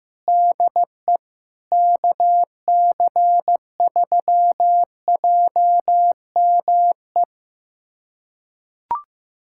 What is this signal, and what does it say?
Morse code, heard as a keyed tone, about 15 wpm, 700 Hz. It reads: DE KC3JME
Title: USA Callsign DE Callsign 15wpm